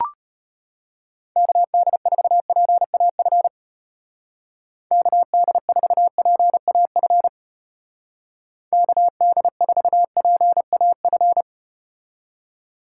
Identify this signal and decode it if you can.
Morse code, heard as a keyed tone, about 33 wpm, 700 Hz. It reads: KD4PAF KD4PAF KD4PAF